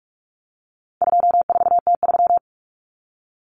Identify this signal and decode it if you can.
Morse code, heard as a keyed tone, about 44 wpm, 700 Hz. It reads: 24T3
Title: Common Cut Numbers - 4-Digit 45wpm